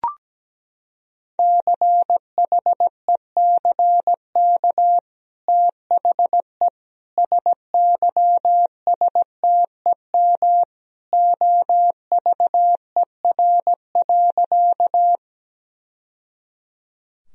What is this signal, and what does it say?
Morse code, heard as a keyed tone, about 17 wpm, 700 Hz. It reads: CHECK THE SYSTEM OVER.
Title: Sentences from Top 1000 Words - Review Part 3 - 17wpm